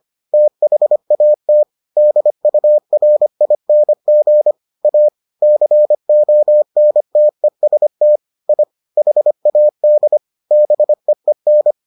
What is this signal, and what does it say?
Morse code, heard as a keyed tone, about 25 wpm, 600 Hz. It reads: THAT DURING A CONTEST I HAD BEEN